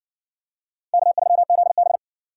Morse code, heard as a keyed tone, about 60 wpm, 700 Hz. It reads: K3ZB